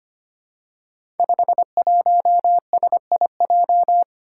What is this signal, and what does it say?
Morse code, heard as a keyed tone, about 25 wpm, 700 Hz. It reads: 51SIJ
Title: Five Challenging Letter-Number Combinations 25wpm